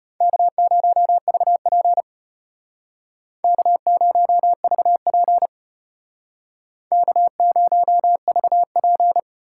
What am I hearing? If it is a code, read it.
Morse code, heard as a keyed tone, about 33 wpm, 700 Hz. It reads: K0VP K0VP K0VP